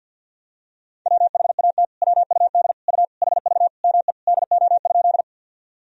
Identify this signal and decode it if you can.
Morse code, heard as a keyed tone, about 46 wpm, 700 Hz. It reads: WHAT WUD U HV ME DO?
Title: Sentences from Top 200 Words and Abbreviations 50wpm